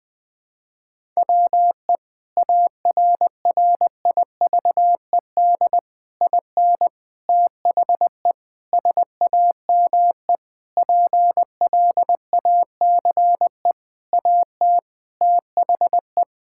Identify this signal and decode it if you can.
Morse code, heard as a keyed tone, about 20 wpm, 700 Hz. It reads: WE ARRIVED IN THE SAME PLACE AT THE
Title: Sentences with 8-Word Phrases - 20wpm